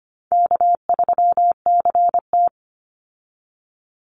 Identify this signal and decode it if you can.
Morse code, heard as a keyed tone, about 25 wpm, 700 Hz. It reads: K3CT
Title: Call Signs Repeated 2x - 25wpm